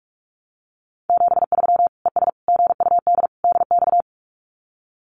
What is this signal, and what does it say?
Morse code, heard as a keyed tone, about 44 wpm, 700 Hz. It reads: 73 ES GUD DX
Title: Common QSO Phrases 45wpm